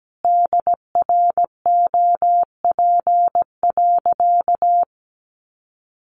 Morse code, heard as a keyed tone, about 17 wpm, 700 Hz. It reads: DROP.
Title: Sentences from Top 1400 Words 17wpm